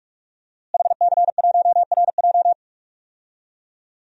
Morse code, heard as a keyed tone, about 46 wpm, 700 Hz. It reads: HC1RJ
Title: DXCC-100 Call Signs - 5 Characters - Repeated 3x - 45wpm